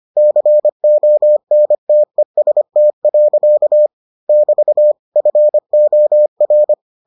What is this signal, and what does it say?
Morse code, heard as a keyed tone, about 25 wpm, 600 Hz. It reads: CONTEST. <BT> FOR